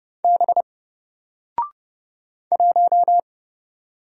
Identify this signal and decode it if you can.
Morse code, heard as a keyed tone, about 31 wpm, 700 Hz. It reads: B 1
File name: Single-Letter-Number-30wpm